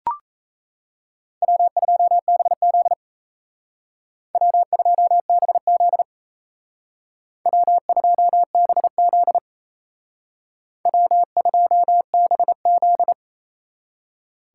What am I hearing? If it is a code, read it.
Morse code, heard as a keyed tone, about 36 wpm, 700 Hz. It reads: W2BZ W2BZ W2BZ W2BZ